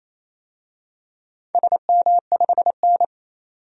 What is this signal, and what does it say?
Morse code, heard as a keyed tone, about 28 wpm, 700 Hz. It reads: SM5N
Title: DXCC-100 Call Signs - 4 Characters - 28wpm